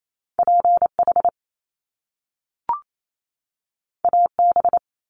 Morse code, heard as a keyed tone, about 28 wpm, 700 Hz. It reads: PH AB
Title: Top Two Letter Combinations in Words 28wpm